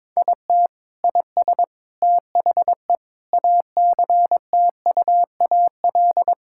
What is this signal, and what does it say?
Morse code, heard as a keyed tone, about 22 wpm, 700 Hz. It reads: IT IS THE ACTUAL
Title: Sentences from Top 1000 Words - Review Part 5 - 22wpm